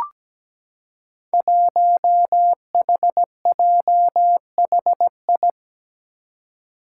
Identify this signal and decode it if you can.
Morse code, heard as a keyed tone, about 17 wpm, 700 Hz. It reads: 1HJHI
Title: Five Challenging Letter-Number Combinations 17wpm